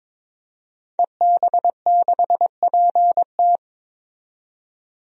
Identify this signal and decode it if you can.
Morse code, heard as a keyed tone, about 22 wpm, 700 Hz. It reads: EB6PT
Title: DXCC-100 Call Signs - 5 Characters - 22wpm